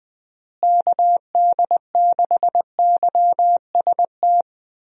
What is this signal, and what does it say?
Morse code, heard as a keyed tone, about 20 wpm, 700 Hz. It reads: KD6YST
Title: USA Call Signs - 6 Characters - Repeated 3x - 20wpm